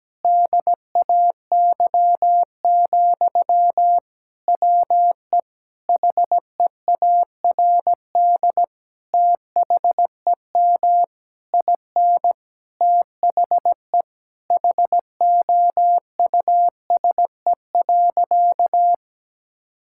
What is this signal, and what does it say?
Morse code, heard as a keyed tone, about 17 wpm, 700 Hz. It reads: DAY, WE HEARD THEM IN THE HOUSE.
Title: Sentences from Top 1000 Words - Review Part 5 - 17wpm